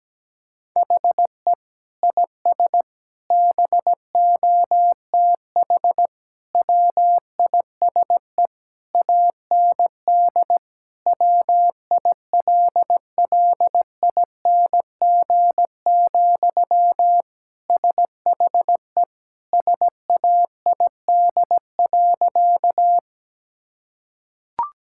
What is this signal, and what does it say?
Morse code, heard as a keyed tone, about 17 wpm, 700 Hz. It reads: HE IS BOTH WISE AND WILLING, SHE SAID.
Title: Sentences from Top 1600 Words 17wpm